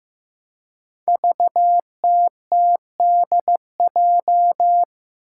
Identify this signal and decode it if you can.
Morse code, heard as a keyed tone, about 15 wpm, 700 Hz. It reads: VTTDJ
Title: Five Challenging Letter-Number Combinations 15wpm